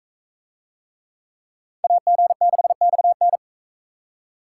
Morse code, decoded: AG6XN

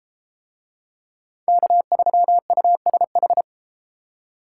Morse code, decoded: K3USH

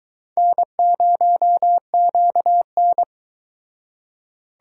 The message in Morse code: N0QN

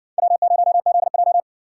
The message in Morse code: W0ZJ

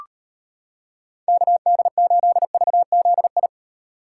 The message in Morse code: KD8VZI

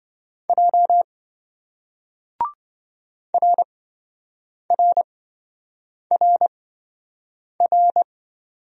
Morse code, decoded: J RRRR

